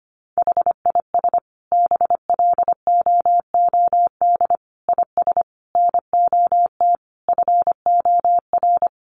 Morse code, heard as a keyed tone, about 25 wpm, 700 Hz. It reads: HIS BLOOD IS NOT FOR
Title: Sentences from Top 1400 Words 25wpm